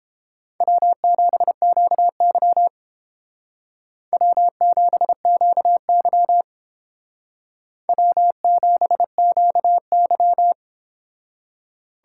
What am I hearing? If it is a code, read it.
Morse code, heard as a keyed tone, about 31 wpm, 700 Hz. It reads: W7QY W7QY W7QY